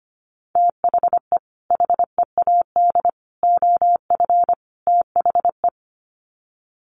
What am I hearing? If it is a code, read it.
Morse code, heard as a keyed tone, about 25 wpm, 700 Hz. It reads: THE HEAD OF THE
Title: Sets of 4 Words 25wpm